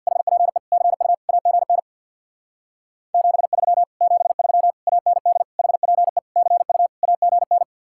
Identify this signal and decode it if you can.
Morse code, heard as a keyed tone, about 64 wpm, 700 Hz. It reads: HPE CU AGN 73 73 AND HPE CU AGN